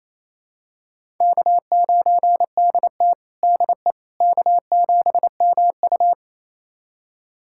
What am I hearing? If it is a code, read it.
Morse code, heard as a keyed tone, about 28 wpm, 700 Hz. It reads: K9DT DE K7MU
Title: USA Callsign DE Callsign - 4 Characters - 28wpm